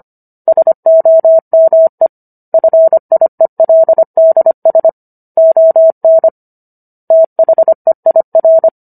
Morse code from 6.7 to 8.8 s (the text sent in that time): THEIR